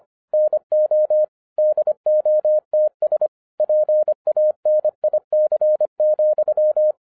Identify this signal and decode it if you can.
Morse code, heard as a keyed tone, about 25 wpm, 610 Hz. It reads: NO DOTS PANIC,
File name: FIRST FISTS CONTACT-25wpm